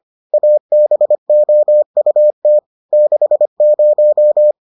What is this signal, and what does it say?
Morse code, heard as a keyed tone, about 25 wpm, 600 Hz. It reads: ABOUT 60